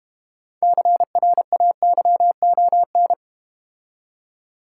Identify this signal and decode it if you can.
Morse code, heard as a keyed tone, about 32 wpm, 690 Hz. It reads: CRAYON